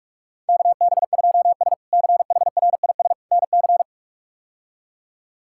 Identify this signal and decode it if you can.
Morse code, heard as a keyed tone, about 46 wpm, 700 Hz. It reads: KB2S CHRIS NC
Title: K1USN SST QSO Practice - 45wpm